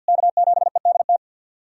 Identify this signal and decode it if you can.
Morse code, heard as a keyed tone, about 47 wpm, 700 Hz. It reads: K7EDT
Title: USA Callsign DE Callsign - 5 Characters - 50wpm